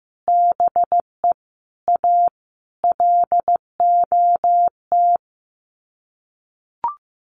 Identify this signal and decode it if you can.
Morse code, heard as a keyed tone, about 15 wpm, 700 Hz. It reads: BE A LOT
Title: Sets of 5 Words Encore 15wpm